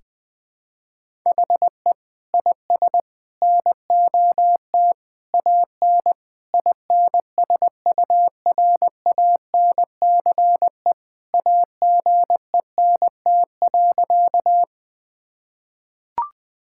Morse code, heard as a keyed tone, about 20 wpm, 700 Hz. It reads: HE IS NOT AN INSURANCE AGENT.
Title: Sentences from Top 1600 Words 20wpm